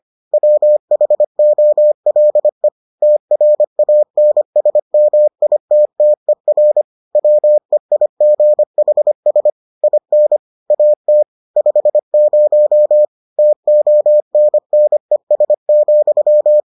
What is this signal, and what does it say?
Morse code, heard as a keyed tone, about 25 wpm, 600 Hz. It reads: WHOLE TRANSMITTER WEIGHS IN AT 50 TONNES,